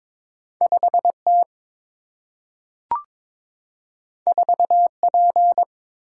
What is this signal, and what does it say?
Morse code, heard as a keyed tone, about 22 wpm, 700 Hz. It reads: 5T 4P
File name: practice / 22wpm / Two Letter-Number Combinations - 22wpm